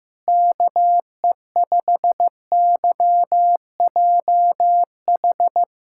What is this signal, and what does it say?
Morse code, heard as a keyed tone, about 15 wpm, 700 Hz. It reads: KE5YJH
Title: USA Callsign DE Callsign 15wpm